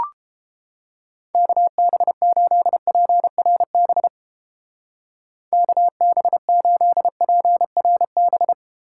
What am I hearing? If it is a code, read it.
Morse code, heard as a keyed tone, about 31 wpm, 700 Hz. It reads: KB8PRB KB8PRB